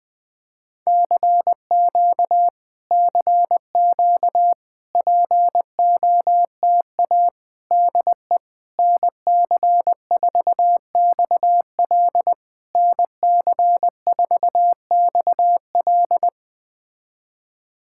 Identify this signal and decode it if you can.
Morse code, heard as a keyed tone, about 20 wpm, 700 Hz. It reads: CQ CQ POTA DE NC4XL NC4XL
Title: POTA QSO Practice - 20wpm